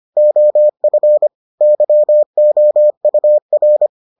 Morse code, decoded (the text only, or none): OF YOUR